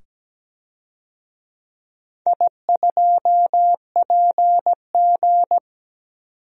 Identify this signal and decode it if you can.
Morse code, heard as a keyed tone, about 17 wpm, 700 Hz. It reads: I2PG
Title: DXCC-100 Call Signs - 4 Characters - 17wpm